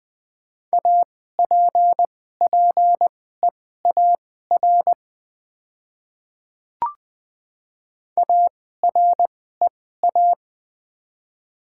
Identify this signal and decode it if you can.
Morse code, heard as a keyed tone, about 20 wpm, spaced as for 14 wpm, 700 Hz. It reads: APPEAR AREA
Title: 2x Character Spacing - Top 500 Words - 20wpm